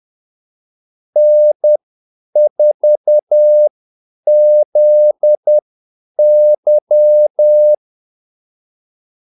Morse code, heard as a keyed tone, about 10 wpm, spaced as for 8 wpm, 600 Hz. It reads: N4ZY
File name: Callsigns-10-8wpm